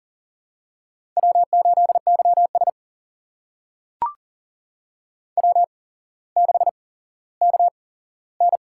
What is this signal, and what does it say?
Morse code, heard as a keyed tone, about 38 wpm, spaced as for 17 wpm, 700 Hz. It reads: W8YS W6KN
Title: USA Call Signs - 4 Characters - 8x Character Spacing - 40wpm